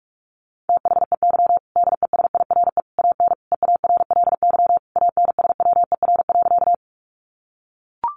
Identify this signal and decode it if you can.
Morse code, heard as a keyed tone, about 44 wpm, 700 Hz. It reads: THEY DESIRE AN EARLY ANSWER.
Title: Sentences from Top 1100 Words 45wpm